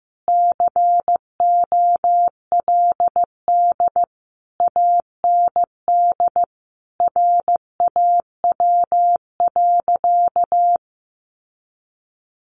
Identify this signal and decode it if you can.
Morse code, heard as a keyed tone, about 15 wpm, 700 Hz. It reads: COLD AND RAW.
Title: Sentences from Top 1500 Words 15wpm